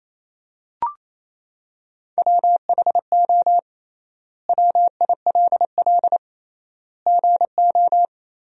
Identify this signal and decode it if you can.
Morse code, heard as a keyed tone, about 28 wpm, 700 Hz. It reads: WHO WILL GO
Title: 3x Word-Spacing - Sentences from Top 100 Words - 28wpm